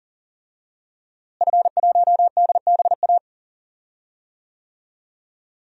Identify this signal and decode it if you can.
Morse code, heard as a keyed tone, about 40 wpm, 700 Hz. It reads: F1DBA
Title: DXCC-100 Call Signs - 5 Characters - Repeated 3x - 40wpm